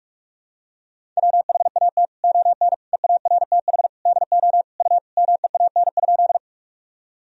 Morse code, heard as a keyed tone, about 46 wpm, 700 Hz. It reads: WHAT ON EARTH DO U MEAN?
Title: Sentences from Top 200 Words and Abbreviations 45wpm